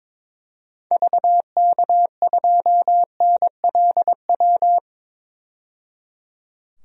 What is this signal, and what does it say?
Morse code, heard as a keyed tone, about 22 wpm, 700 Hz. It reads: VK2NLW
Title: DXCC-100 Call Signs - 6 Characters - 22wpm